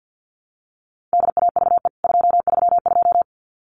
Morse code, heard as a keyed tone, about 50 wpm, 700 Hz. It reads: DAVE 232
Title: CWT Exchange 50wpm